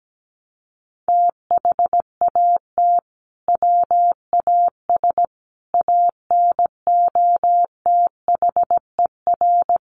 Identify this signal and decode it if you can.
Morse code, heard as a keyed tone, about 17 wpm, 700 Hz. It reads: THAT WAS ANOTHER
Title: Sentences from Top 600 Words 17wpm